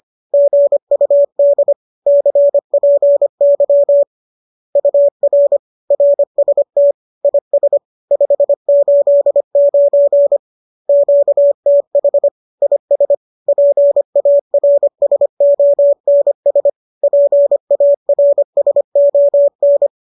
Morse code, heard as a keyed tone, about 25 wpm, 570 Hz. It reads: GUD CPY UR RST IS 589 QTH IS PARSONS PARSON
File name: qso205 25